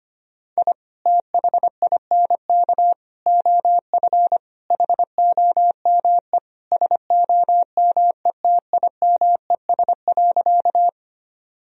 Morse code, heard as a keyed tone, about 25 wpm, 700 Hz. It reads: I THINK OF HOME SOMETIMES.
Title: Sentences from Top 1000 Words - Review Part 3 - 25wpm